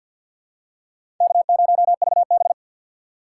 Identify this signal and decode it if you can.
Morse code, heard as a keyed tone, about 48 wpm, 700 Hz. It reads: K0VB
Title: USA Call Signs - 4 Characters - 2x Character Spacing - 50wpm